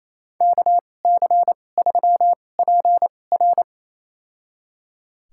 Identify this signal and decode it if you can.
Morse code, heard as a keyed tone, about 28 wpm, spaced as for 20 wpm, 700 Hz. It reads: KC3PR
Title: USA Call Signs - 2x Character Spacing - 28wpm